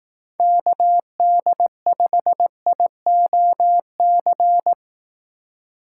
Morse code, heard as a keyed tone, about 18 wpm, 700 Hz. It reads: KD5IOC
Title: Speed Racing - USA 2x3 Call Signs 15wpm